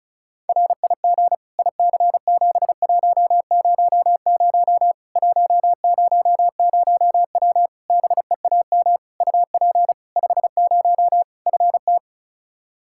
Morse code, heard as a keyed tone, about 36 wpm, 700 Hz. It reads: RIG IC7100 100W BEAM UP 50 FT
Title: Practice QSO - Exchange 2 - Info - 35wpm